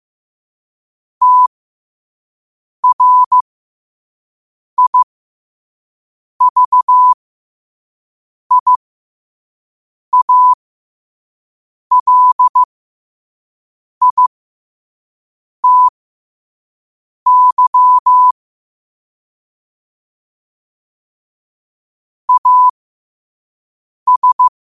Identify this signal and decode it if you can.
Morse code, heard as a keyed tone, about 15 wpm, spaced as for 5 wpm, 1000 Hz. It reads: TRIVIALITY AS
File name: qotdfiles / 5wpm